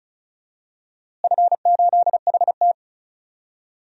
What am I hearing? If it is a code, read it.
Morse code, heard as a keyed tone, about 34 wpm, 700 Hz. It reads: F8HT
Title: DXCC-100 Call Signs - 4 Characters - Repeated 2x - 35wpm